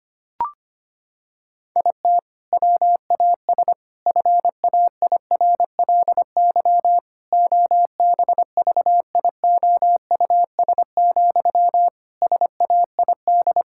Morse code, heard as a keyed tone, about 25 wpm, 700 Hz. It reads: IT WAS FAIRLY OBVIOUS, SAID
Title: Sentences from Top 1600 Words 25wpm